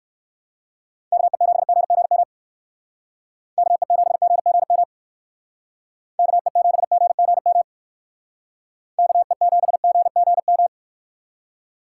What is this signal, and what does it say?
Morse code, heard as a keyed tone, about 60 wpm, 700 Hz. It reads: KE7GGM KE7GGM KE7GGM KE7GGM